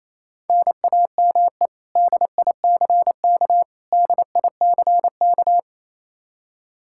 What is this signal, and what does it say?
Morse code, heard as a keyed tone, about 27 wpm, 700 Hz. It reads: NAME DICK DICK
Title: Practice QSO - Exchange 1 - Beginning - 28wpm